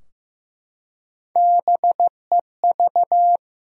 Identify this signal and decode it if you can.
Morse code, heard as a keyed tone, about 15 wpm, 700 Hz. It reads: BEV
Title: USA Common Names - 15wpm